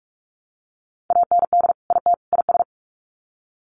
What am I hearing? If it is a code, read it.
Morse code, heard as a keyed tone, about 44 wpm, 700 Hz. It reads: AND IT IS
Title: Sets of 3 Words 45wpm